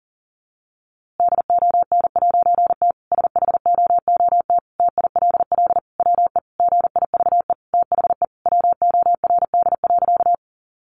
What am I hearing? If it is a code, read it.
Morse code, heard as a keyed tone, about 37 wpm, 700 Hz. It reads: DON'T SHOOT TILL WE GIVE THE WORD.
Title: A Sound Of Thunder 40wpm